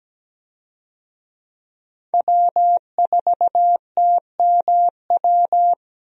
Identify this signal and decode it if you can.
Morse code, heard as a keyed tone, about 17 wpm, 700 Hz. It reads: W4TMW
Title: USA Call Signs - 5 Characters - Repeated 2x - 17wpm